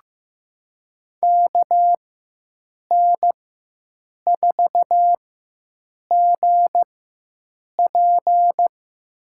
Morse code, heard as a keyed tone, about 15 wpm, spaced as for 7 wpm, 700 Hz. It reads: KN4GP
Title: USA Call Signs - 5 Characters - 4x Character Spacing - 15wpm